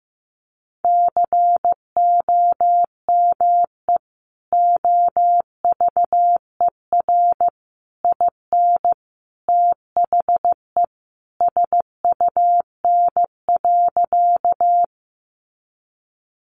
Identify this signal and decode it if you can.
Morse code, heard as a keyed tone, about 15 wpm, 700 Hz. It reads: COME OVER IN THE SUN.
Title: Sentences from Top 300 Words 15wpm